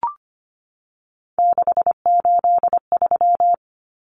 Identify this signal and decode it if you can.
Morse code, heard as a keyed tone, about 25 wpm, 700 Hz. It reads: 683
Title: Numbers - 3-Digit 25wpm